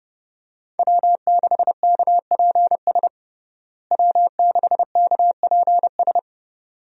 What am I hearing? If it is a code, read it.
Morse code, heard as a keyed tone, about 32 wpm, 700 Hz. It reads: W6KPS W6KPS
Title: USA Call Signs - 5 Characters - Repeated 2x - 30wpm